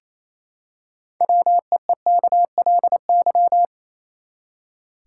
Morse code, heard as a keyed tone, about 28 wpm, 700 Hz. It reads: WEEKLY
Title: Six Letter Words - 28wpm